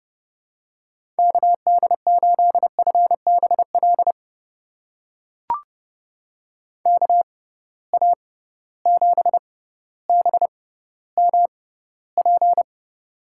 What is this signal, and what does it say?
Morse code, heard as a keyed tone, about 30 wpm, spaced as for 15 wpm, 700 Hz. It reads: KD8FBL KA7BMP
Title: USA Call Signs - 6 Characters - 6x Character Spacing - 30wpm